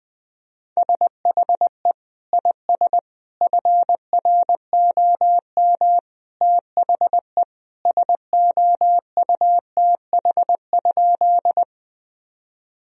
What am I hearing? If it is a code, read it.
Morse code, heard as a keyed tone, about 20 wpm, 700 Hz. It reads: SHE IS FROM THE SOUTH?
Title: Sentences from Top 1000 Words - Review Part 2 - 20wpm